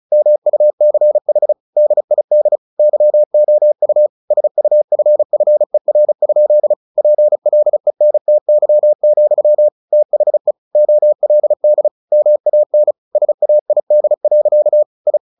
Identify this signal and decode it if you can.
Morse code, heard as a keyed tone, about 34 wpm, 600 Hz. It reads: MUCH DID YOU SUFFER? PLENTY, THE OLD MAN SAID. I
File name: the old man and the sea 2 35wpm